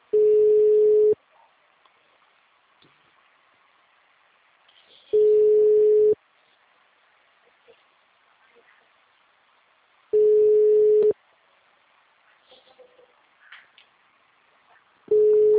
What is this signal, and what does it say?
Ringback tone (Europe, 425 Hz), with speech.
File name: out- -2004- - - .537